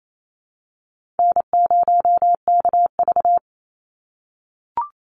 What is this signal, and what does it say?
Morse code, heard as a keyed tone, about 28 wpm, 700 Hz. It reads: N0KV
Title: Easy Call Signs - 28wpm